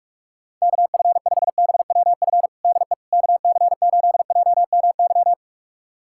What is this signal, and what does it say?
Morse code, heard as a keyed tone, about 46 wpm, 700 Hz. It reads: KF5BWF DE KC8JMY
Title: USA Callsign DE Callsign - 6 Characters - 45wpm